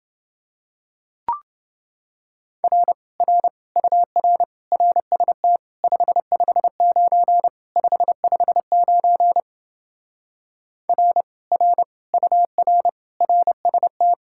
Morse code, heard as a keyed tone, about 30 wpm, 700 Hz. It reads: R R UR RST 559 559 R R UR RST